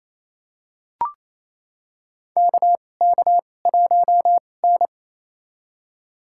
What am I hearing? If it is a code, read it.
Morse code, heard as a keyed tone, about 28 wpm, spaced as for 20 wpm, 700 Hz. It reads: KK1N
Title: USA Call Signs - 4 Characters - 2x Character Spacing - 28wpm